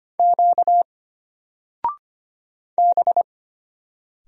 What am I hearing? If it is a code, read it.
Morse code, heard as a keyed tone, about 25 wpm, 700 Hz. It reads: Q B
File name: Single-Letter-Number-25wpm